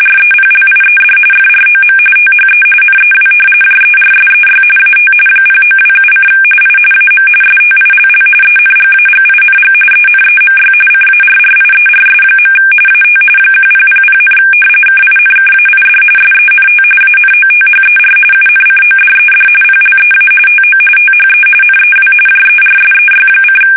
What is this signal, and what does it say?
RTTY, about 45.45 baud. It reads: REPLIED FROM THE NEW JERSEY AT 0927: I AM STILL ENGAGING ENEMY CARRIERS. AT 1000, KINKAID SENT ANOTHER UNCODED PLEA TO THE NEW JERSEY:
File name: fsk850 / battleshipnj